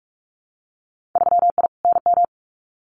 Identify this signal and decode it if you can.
Morse code, heard as a keyed tone, about 46 wpm, 690 Hz. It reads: 3I NM
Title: Winter Field Day Exchanges Encore 45wpm